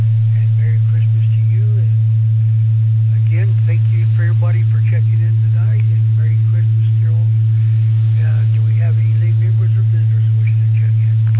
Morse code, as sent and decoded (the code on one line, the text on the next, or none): none